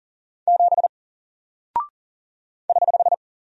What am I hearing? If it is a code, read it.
Morse code, heard as a keyed tone, about 40 wpm, 700 Hz. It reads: 7 <HH>
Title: Single Character (Letter:Number:Punctuation:Prosign) Alt 40wpm